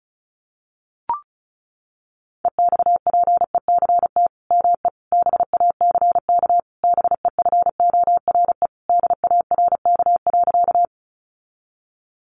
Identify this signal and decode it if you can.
Morse code, heard as a keyed tone, about 34 wpm, 700 Hz. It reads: EXPECT ME BACK BEFORE DARK.
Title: Sentences from Top 1100 Words 35wpm